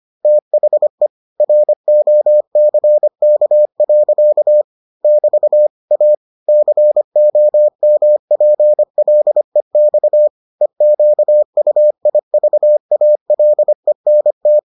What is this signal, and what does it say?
Morse code, heard as a keyed tone, about 25 wpm, 600 Hz. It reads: THE ROCK. <BT> A COMPLEX EQUIVALENT